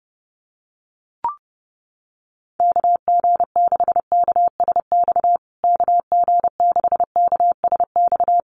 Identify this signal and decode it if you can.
Morse code, heard as a keyed tone, about 30 wpm, 700 Hz. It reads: KG6KSX KG6KSX
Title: USA Call Signs Repeated 3x 30wpm